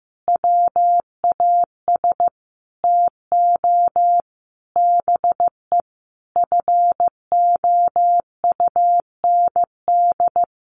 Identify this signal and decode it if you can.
Morse code, heard as a keyed tone, about 14 wpm, 700 Hz. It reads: WAS TO BE FOUND
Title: Sentences from Top 1400 Words 15wpm